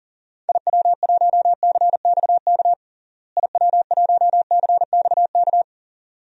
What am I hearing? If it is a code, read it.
Morse code, heard as a keyed tone, about 40 wpm, 700 Hz. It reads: IW1CXK IW1CXK
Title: DXCC-100 Call Signs - 6 Characters - Repeated 2x - 40wpm